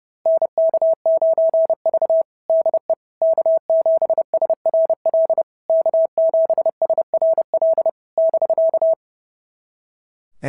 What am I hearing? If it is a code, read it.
Morse code, heard as a keyed tone, about 28 wpm, 650 Hz. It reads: NK9V DE K7SRL K7SRL <BK>
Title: SOTA QSO Practice 30wpm